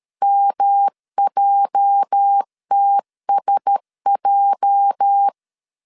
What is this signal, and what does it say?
Morse code, heard as a keyed tone, about 13 wpm, 800 Hz. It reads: MJTSJ
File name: LEZIONI / 8 YJ45 / 8 YJ 12wpm 2235